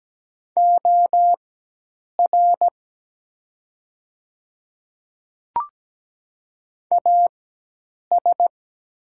Morse code, heard as a keyed tone, about 17 wpm, spaced as for 8 wpm, 700 Hz. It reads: OR AS
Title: 4x Character Spacing - Top 200 Words - 17wpm